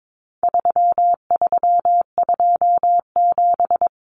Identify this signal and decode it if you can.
Morse code, heard as a keyed tone, about 22 wpm, 700 Hz. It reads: 3327
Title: Common Cut Numbers - 4-Digit 22wpm